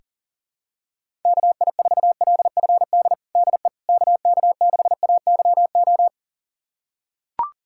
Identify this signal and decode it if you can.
Morse code, heard as a keyed tone, about 37 wpm, 700 Hz. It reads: KI4LFD DE KK6AYO
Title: USA Callsign DE Callsign 40wpm